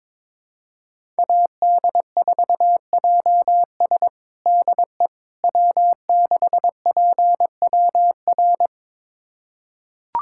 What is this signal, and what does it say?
Morse code, heard as a keyed tone, about 22 wpm, 700 Hz. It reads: AD4JS DE W6PWR
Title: USA Callsign DE Callsign - 5 Characters - 22wpm